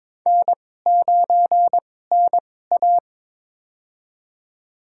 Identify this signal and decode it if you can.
Morse code, heard as a keyed tone, about 22 wpm, spaced as for 16 wpm, 700 Hz. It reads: N9NA
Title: USA Call Signs - 4 Characters - 2x Character Spacing - 22wpm